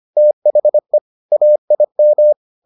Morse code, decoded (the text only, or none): THE AIM